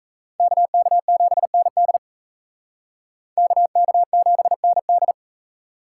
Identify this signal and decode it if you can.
Morse code, heard as a keyed tone, about 36 wpm, 700 Hz. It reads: KK7ND KK7ND